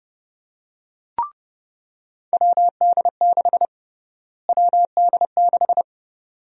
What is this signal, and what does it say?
Morse code, heard as a keyed tone, about 30 wpm, 700 Hz. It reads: WD6 WD6
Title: USA Call Sign Prefixes - Repeated 2x - 30wpm